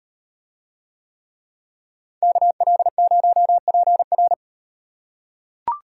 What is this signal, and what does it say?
Morse code, heard as a keyed tone, about 35 wpm, 700 Hz. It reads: KL0PR